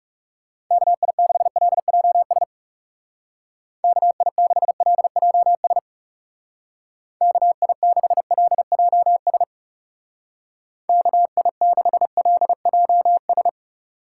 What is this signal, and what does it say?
Morse code, heard as a keyed tone, about 35 wpm, 700 Hz. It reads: KI6LJS KI6LJS KI6LJS KI6LJS
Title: Speed Racing - USA 2x3 Call Signs 30wpm